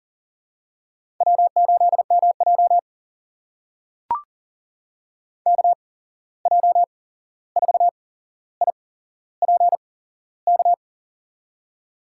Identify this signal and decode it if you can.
Morse code, heard as a keyed tone, about 40 wpm, spaced as for 15 wpm, 700 Hz. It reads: W8MJ KJ4IPK
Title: USA Call Signs - 8x Character Spacing - 40wpm